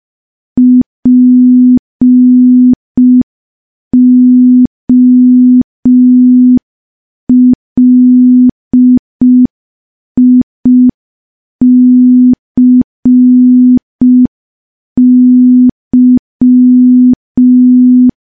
Morse code, heard as a keyed tone, about 5 wpm, 250 Hz. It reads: POLICY
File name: morse / morse-training-05WPM